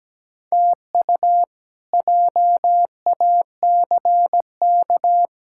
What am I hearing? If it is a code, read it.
Morse code, heard as a keyed tone, about 17 wpm, 700 Hz. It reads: TU JACK
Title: K1USN SST QSO Practice - 17wpm